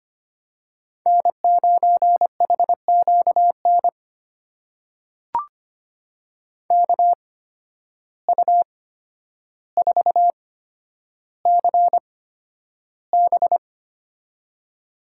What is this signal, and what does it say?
Morse code, heard as a keyed tone, about 25 wpm, spaced as for 11 wpm, 700 Hz. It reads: N9HQN KU4CB
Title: USA Call Signs - 5 Characters - 8x Character Spacing - 25wpm